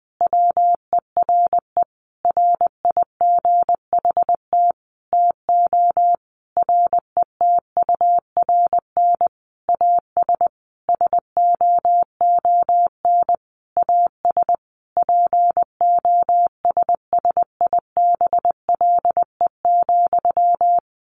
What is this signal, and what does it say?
Morse code, heard as a keyed tone, about 20 wpm, 700 Hz. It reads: WERE RIGHT TO RETURN AS SOON AS POSSIBLE,
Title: Sentences from Top 800 Words 20wpm